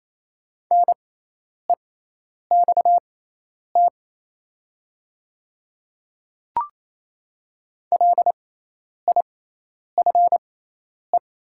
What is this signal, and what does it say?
Morse code, heard as a keyed tone, about 28 wpm, spaced as for 10 wpm, 700 Hz. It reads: NEXT LIFE
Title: 6x Character Spacing - Top 300 Words - 28wpm